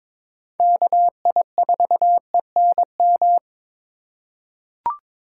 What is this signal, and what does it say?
Morse code, heard as a keyed tone, about 21 wpm, 700 Hz. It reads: KI4ENM
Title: USA Call Signs - 6 Characters - 6x Character Spacing - 22wpm